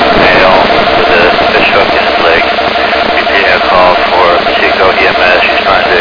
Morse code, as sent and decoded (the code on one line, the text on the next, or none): none